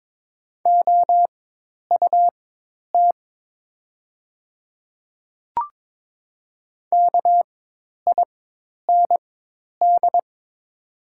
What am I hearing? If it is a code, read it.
Morse code, heard as a keyed tone, about 22 wpm, spaced as for 10 wpm, 700 Hz. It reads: OUT KIND
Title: 4x Character Spacing - Top 300 Words - 22wpm